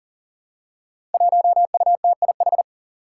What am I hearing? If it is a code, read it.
Morse code, heard as a keyed tone, about 42 wpm, 700 Hz. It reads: 1UTIH
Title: Five Challenging Letter-Number Combinations 40wpm